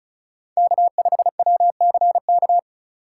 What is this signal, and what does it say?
Morse code, heard as a keyed tone, about 34 wpm, 700 Hz. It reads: K5WCK
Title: USA Call Signs - 5 Characters - 3x Character Spacing - 35wpm